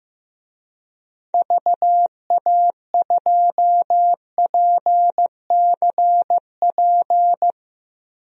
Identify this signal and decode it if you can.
Morse code, heard as a keyed tone, about 15 wpm, 700 Hz. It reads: VA2PCP
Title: DXCC-100 Call Signs - 6 Characters - Repeated 3x - 15wpm